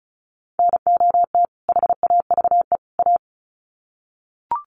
Morse code, heard as a keyed tone, about 34 wpm, 700 Hz. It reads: NOT HAVE A
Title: Sets of 3 Words 35wpm